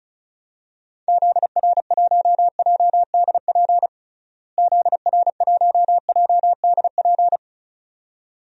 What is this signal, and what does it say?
Morse code, heard as a keyed tone, about 34 wpm, 700 Hz. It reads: ZR1JDP ZR1JDP
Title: DXCC-100 Call Signs - 6 Characters - Repeated 2x - 35wpm